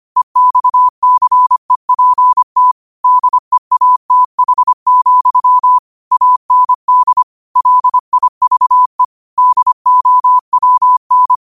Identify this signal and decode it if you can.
Morse code, heard as a keyed tone, about 25 wpm, 1000 Hz. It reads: EXCEPT DEATH, AND LIVE DOWN
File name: qotdfiles / 25wpm